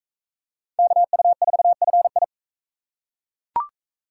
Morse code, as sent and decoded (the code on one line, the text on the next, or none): -.- ..- ....- ..-. ..
KU4FI